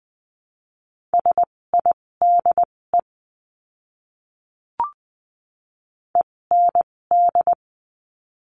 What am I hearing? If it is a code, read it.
Morse code, heard as a keyed tone, about 20 wpm, spaced as for 16 wpm, 700 Hz. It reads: SIDE END
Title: 256 Lesson 42a - Words 12-20wpm